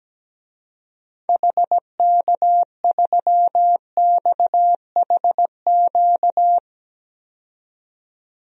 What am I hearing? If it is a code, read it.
Morse code, heard as a keyed tone, about 17 wpm, 700 Hz. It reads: HK3XHQ
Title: DXCC-100 Call Signs - 6 Characters - 17wpm